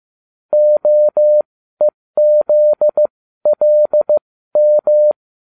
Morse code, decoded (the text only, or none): OEZLM